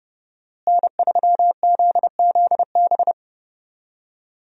Morse code, heard as a keyed tone, about 30 wpm, 700 Hz. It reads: N3ZZB
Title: USA Call Signs - 5 Characters - Repeated 3x - 30wpm